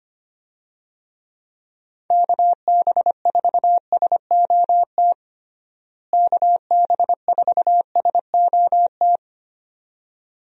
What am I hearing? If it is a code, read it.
Morse code, heard as a keyed tone, about 25 wpm, 700 Hz. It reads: KB4SOT KB4SOT
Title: USA Call Signs - 6 Characters - Repeated 2x - 25wpm